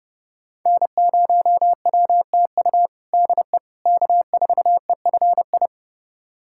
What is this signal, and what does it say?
Morse code, heard as a keyed tone, about 30 wpm, 700 Hz. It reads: N0WTU DE K4EFI
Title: USA Callsign DE Callsign - 5 Characters - 30wpm